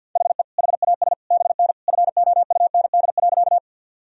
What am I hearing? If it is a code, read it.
Morse code, heard as a keyed tone, about 52 wpm, 700 Hz. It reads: HE HAS BN FOUND.
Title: Sentences from Top 300 Words and Abbreviations 50wpm